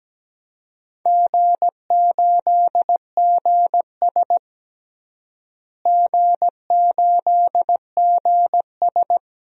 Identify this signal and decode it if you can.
Morse code, heard as a keyed tone, about 17 wpm, 700 Hz. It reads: G8GS G8GS
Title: DXCC-100 Call Signs - 4 Characters - Repeated 3x - 17wpm